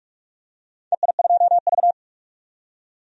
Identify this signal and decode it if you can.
Morse code, heard as a keyed tone, about 42 wpm, 700 Hz. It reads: EI2V
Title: DXCC-100 Call Signs - 4 Characters - 45wpm